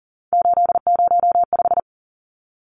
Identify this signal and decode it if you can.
Morse code, heard as a keyed tone, about 40 wpm, 700 Hz. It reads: 805
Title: Numbers - 3-Digit 40wpm